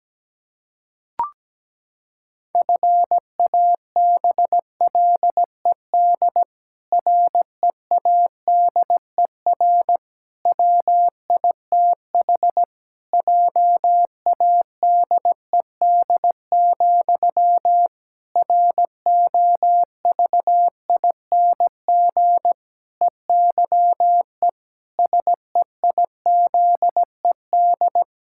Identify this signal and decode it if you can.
Morse code, heard as a keyed tone, about 17 wpm, 700 Hz. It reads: FABLED READER WITH JADED, ROVING EYE SEIZED
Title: Pangram Challenge v2 - 17wpm